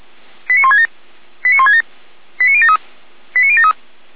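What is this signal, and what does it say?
Five-tone selective call (ZVEI), heard as quick runs of five tones, tones of about 70 ms. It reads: 89167 89167 89073 89073
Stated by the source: ZVEI five-tone call received on an SDR, from audio.f85360.feuerwehr-orlamuende.org (Feuerwehr Orlamünde fire brigade).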